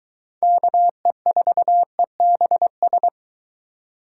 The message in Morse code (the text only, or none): KE4EBS